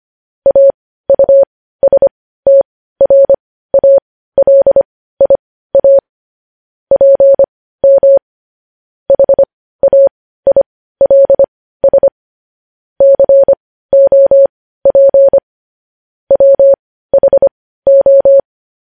Morse code, heard as a keyed tone, about 25 wpm, spaced as for 15 wpm, 550 Hz. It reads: AUSTRALIA PM HAILS COP WHO
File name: podcast / News headlines-1x-25x15